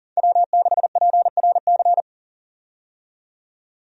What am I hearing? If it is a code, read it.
Morse code, heard as a keyed tone, about 40 wpm, 700 Hz. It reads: W6PRC
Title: USA Call Signs - 5 Characters - 40wpm